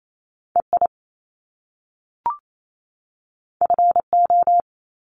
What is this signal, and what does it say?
Morse code, heard as a keyed tone, about 28 wpm, 700 Hz. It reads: EI FO
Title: Top Two Letter Combinations in Words 28wpm